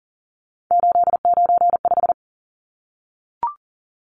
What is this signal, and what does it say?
Morse code, heard as a keyed tone, about 40 wpm, 700 Hz. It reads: 895
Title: Numbers - 3-Digit 40wpm